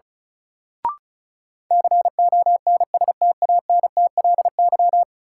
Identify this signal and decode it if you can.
Morse code, heard as a keyed tone, about 36 wpm, 700 Hz. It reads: CONSTANTLY
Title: Sentences from Top 1000 Words - Review Part 7 - New Word Drill - 35wpm